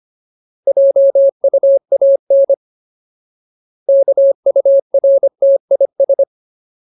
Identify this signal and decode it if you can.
Morse code, heard as a keyed tone, about 25 wpm, 550 Hz. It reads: JUAN KURTIS
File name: Ultimate Word List / 25WPM Ultimate